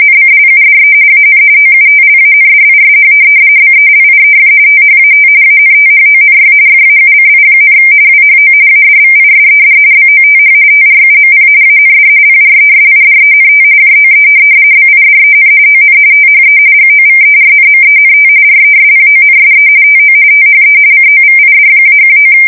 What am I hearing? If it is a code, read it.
RTTY, about 45.45 baud. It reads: NCLUDE AN OUTLANDISH BELIEF THAT ANTHONY FAUCI, WHO ADVISED BOTH TRUMP AND PRESIDENT JOE BIDEN ON THE GOVERNMENT RESPONSE TO THE COVID-19
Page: 170 Shift 2125/2295